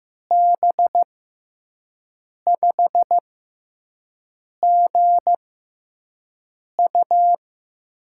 Morse code, heard as a keyed tone, about 15 wpm, spaced as for 5 wpm, 700 Hz. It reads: B5GU
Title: USA Call Signs - 6x Character Spacing - 15wpm